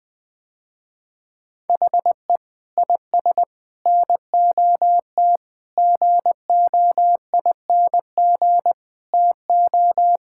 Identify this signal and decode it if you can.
Morse code, heard as a keyed tone, about 20 wpm, 700 Hz. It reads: HE IS NOT GOING TO
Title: Sentences with 4-Word Phrases - Spoken Beginning - 20wpm